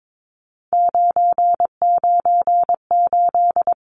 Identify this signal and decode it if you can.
Morse code, heard as a keyed tone, about 22 wpm, 700 Hz. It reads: 998
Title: Numbers - 3-Digit 22wpm